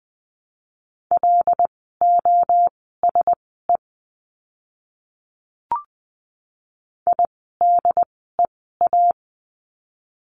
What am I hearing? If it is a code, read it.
Morse code, heard as a keyed tone, about 20 wpm, spaced as for 14 wpm, 700 Hz. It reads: LOSE IDEA
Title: 108 Lesson 18f - Letter m - Words 20wpm